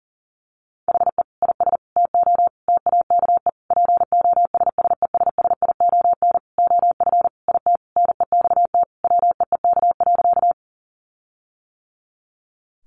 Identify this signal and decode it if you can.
Morse code, heard as a keyed tone, about 40 wpm, 700 Hz. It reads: HE IS TO TAKE POSSESSION OF IT NEXT WEEK.
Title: Sentences from Top 1400 Words 40wpm